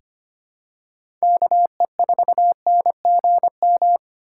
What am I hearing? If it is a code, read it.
Morse code, heard as a keyed tone, about 25 wpm, 700 Hz. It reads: KE4NGM
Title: USA Call Signs - 6 Characters - 2x Character Spacing - 25wpm